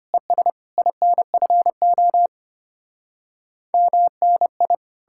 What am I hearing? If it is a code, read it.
Morse code, heard as a keyed tone, about 33 wpm, 700 Hz. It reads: ES INFO MNI